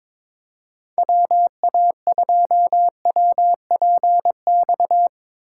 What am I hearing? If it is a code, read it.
Morse code, heard as a keyed tone, about 24 wpm, 700 Hz. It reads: WA2WPX
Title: Speed Racing - USA Call Signs 22wpm